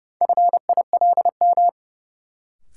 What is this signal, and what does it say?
Morse code, heard as a keyed tone, about 31 wpm, 700 Hz. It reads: FILM
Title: Sentences from Top 1000 Words - Review Part 3 - New Word Drill - 30wpm